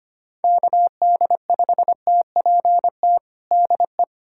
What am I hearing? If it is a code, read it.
Morse code, heard as a keyed tone, about 25 wpm, 700 Hz. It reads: KD5TPT DE
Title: USA Callsign DE Callsign 25wpm